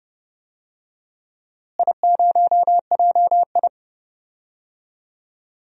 Morse code, heard as a keyed tone, about 31 wpm, 700 Hz. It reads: I0JI
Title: DXCC-100 Call Signs - 4 Characters - Repeated 3x - 30wpm